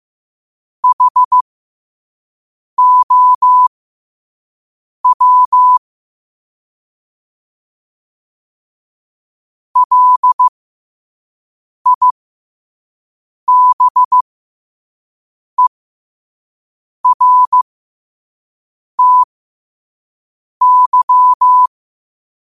Morse code, heard as a keyed tone, about 15 wpm, spaced as for 5 wpm, 1000 Hz. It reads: HOW LIBERTY
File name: qotdfiles / 5wpm